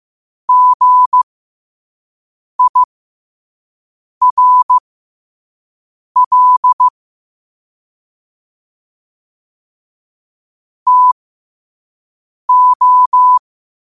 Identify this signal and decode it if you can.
Morse code, heard as a keyed tone, about 15 wpm, spaced as for 5 wpm, 1000 Hz. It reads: GIRL TO